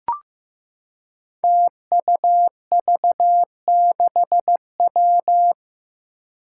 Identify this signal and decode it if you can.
Morse code, heard as a keyed tone, about 15 wpm, 700 Hz. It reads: TUV6W
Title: Five Challenging Letter-Number Combinations 15wpm